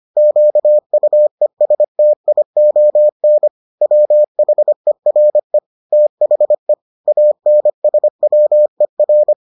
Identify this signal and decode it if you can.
Morse code, heard as a keyed tone, about 25 wpm, 600 Hz. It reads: QUESTION WHERE THE ANSWER